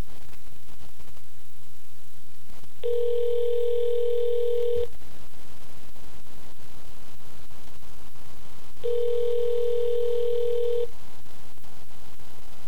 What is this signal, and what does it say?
Ringback tone (North America), 440 + 480 Hz.